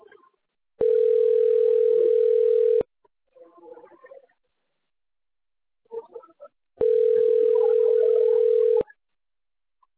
Ringback tone (North America), 440 + 480 Hz.